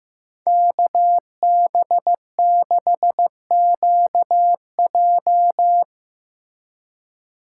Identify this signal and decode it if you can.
Morse code, heard as a keyed tone, about 16 wpm, 700 Hz. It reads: KB6QJ
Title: Speed Racing - USA 2x2 Call Signs 15wpm